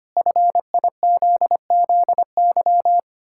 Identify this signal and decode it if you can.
Morse code, heard as a keyed tone, about 25 wpm, 700 Hz. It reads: FIZZY
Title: Pangram Challenge v2 - 25wpm